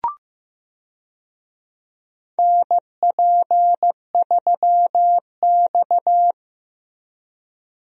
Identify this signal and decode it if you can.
Morse code, heard as a keyed tone, about 15 wpm, 700 Hz. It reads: NP3X
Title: USA Call Signs - 4 Characters - Repeated 2x - 15wpm